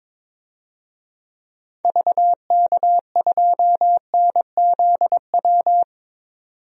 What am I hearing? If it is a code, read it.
Morse code, heard as a keyed tone, about 22 wpm, 700 Hz. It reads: VK2NZW
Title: DXCC-100 Call Signs - 6 Characters - 22wpm